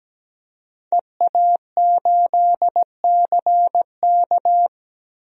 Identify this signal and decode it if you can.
Morse code, heard as a keyed tone, about 17 wpm, 700 Hz. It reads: EA8CK
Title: DXCC-100 Call Signs - 5 Characters - Repeated 2x - 17wpm